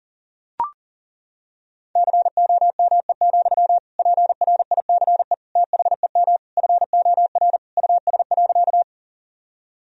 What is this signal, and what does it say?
Morse code, heard as a keyed tone, about 37 wpm, 700 Hz. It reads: COME, PRICE THEM FOR US.
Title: Sentences from Top 1000 Words - Review Part 4 - 40wpm